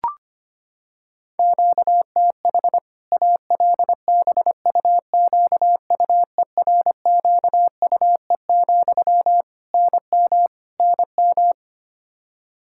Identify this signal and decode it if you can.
Morse code, heard as a keyed tone, about 25 wpm, 700 Hz. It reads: QTH ALBUQUERQUE, NM NM
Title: K1USN SST QSO Practice 25wpm